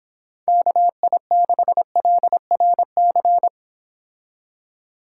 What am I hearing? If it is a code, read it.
Morse code, heard as a keyed tone, about 26 wpm, 700 Hz. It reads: KI6LRC